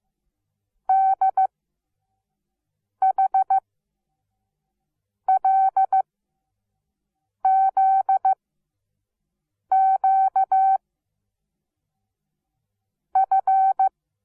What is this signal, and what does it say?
Morse code, heard as a keyed tone, about 15 wpm, 770 Hz. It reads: D H L Z Q F
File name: Onlinelehrgang / morsen / DJ4UF Morsen17